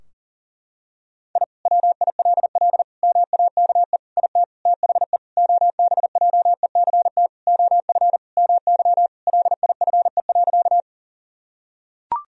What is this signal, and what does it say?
Morse code, heard as a keyed tone, about 37 wpm, 700 Hz. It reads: I WILL MAKE IT THE OBJECT OF MY LIFE.
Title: Sentences from Top 1000 Words - Review Part 3 - 40wpm